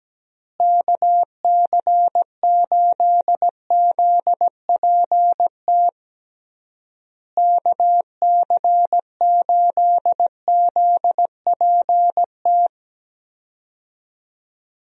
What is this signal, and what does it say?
Morse code, heard as a keyed tone, about 17 wpm, 700 Hz. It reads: KC8ZPT KC8ZPT
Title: USA Call Signs - 6 Characters - Repeated 3x - 17wpm